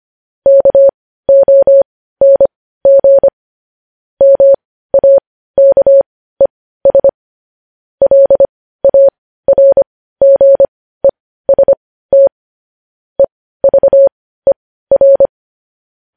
Morse code, heard as a keyed tone, about 25 wpm, spaced as for 15 wpm, 550 Hz. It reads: KONG MAKES LARGEST EVER
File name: podcast / News headlines-1x-25x15